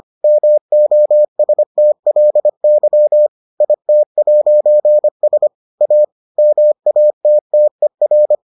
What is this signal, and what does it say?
Morse code, heard as a keyed tone, about 25 wpm, 600 Hz. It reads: MOSTLY IT'S A MATTER